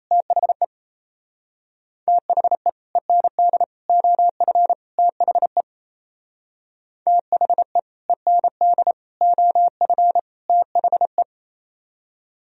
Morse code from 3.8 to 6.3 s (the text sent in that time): OF THE